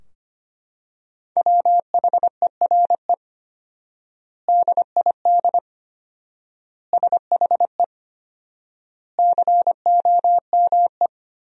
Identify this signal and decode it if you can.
Morse code, heard as a keyed tone, about 25 wpm, 700 Hz. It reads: WHERE DID SHE COME
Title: 4x Word-Spacing - Sentences from Top 200 Words - 25wpm